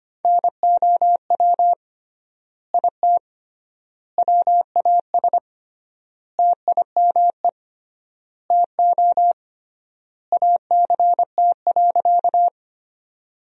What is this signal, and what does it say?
Morse code, heard as a keyed tone, about 25 wpm, 700 Hz. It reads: NOW IT WAS TIME TO ACT.
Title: 3x Word-Spacing - Sentences from Top 200 Words - 25wpm